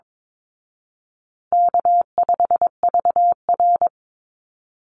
Morse code, heard as a keyed tone, about 22 wpm, 700 Hz. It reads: K5VR
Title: Easy Call Signs Repeated 3x - 22wpm